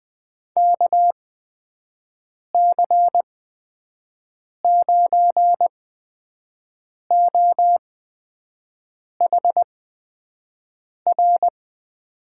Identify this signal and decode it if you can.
Morse code, heard as a keyed tone, about 20 wpm, spaced as for 5 wpm, 700 Hz. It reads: KC9OHR
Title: USA Call Signs - 6 Characters - 8x Character Spacing - 20wpm